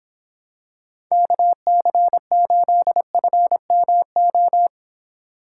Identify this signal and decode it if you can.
Morse code, heard as a keyed tone, about 26 wpm, 700 Hz. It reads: KC8FMO